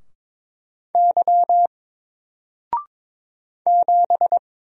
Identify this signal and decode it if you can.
Morse code, heard as a keyed tone, about 22 wpm, 700 Hz. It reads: Y 7
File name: Single-Letter-Number-22wpm